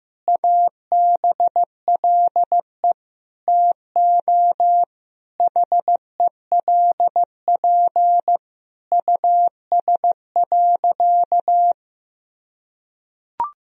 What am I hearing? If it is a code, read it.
Morse code, heard as a keyed tone, about 15 wpm, 700 Hz. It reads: ABLE TO HELP US.
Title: Sentences from Top 1000 Words - Review Part 2 - 15wpm